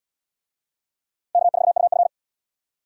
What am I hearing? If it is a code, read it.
Morse code, heard as a keyed tone, about 78 wpm, 700 Hz. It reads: K4UF